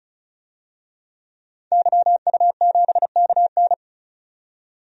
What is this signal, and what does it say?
Morse code, heard as a keyed tone, about 36 wpm, 700 Hz. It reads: YU7KN
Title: DXCC-100 Call Signs - 5 Characters - 35wpm